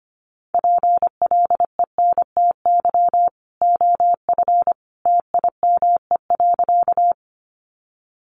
Morse code, heard as a keyed tone, about 25 wpm, 700 Hz. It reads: PLENTY OF TIME.
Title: Sentences from Top 1300 Words 25wpm